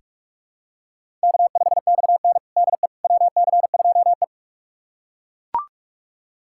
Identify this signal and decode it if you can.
Morse code, heard as a keyed tone, about 42 wpm, 700 Hz. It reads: K5XN DE WC2E
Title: USA Callsign DE Callsign - 4 Characters - 45wpm